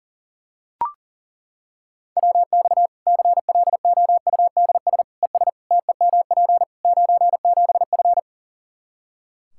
Morse code, heard as a keyed tone, about 37 wpm, 700 Hz. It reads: WX CLOUDS ES TEMP 97F
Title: Practice QSO - Exchange 2 - Info - 40wpm